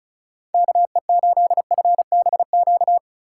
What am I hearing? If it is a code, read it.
Morse code, heard as a keyed tone, about 36 wpm, 700 Hz. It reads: KE8FBQ
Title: USA Call Signs - 6 Characters - 4x Character Spacing - 35wpm